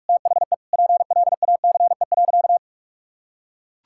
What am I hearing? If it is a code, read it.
Morse code, heard as a keyed tone, about 43 wpm, 700 Hz. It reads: THE PLACE.
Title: Sentences from Top 1600 Words 45wpm